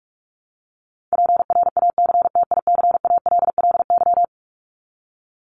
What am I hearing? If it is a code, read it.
Morse code, heard as a keyed tone, about 44 wpm, 700 Hz. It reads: PRACTICALLY
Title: LY Suffix - Top 100 Words with Base Words 45wpm